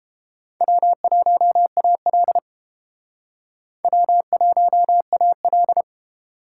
Morse code, heard as a keyed tone, about 31 wpm, 700 Hz. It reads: W1AL W1AL